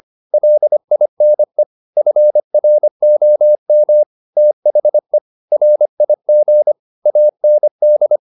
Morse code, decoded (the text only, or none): LINE FROM THE RIG AND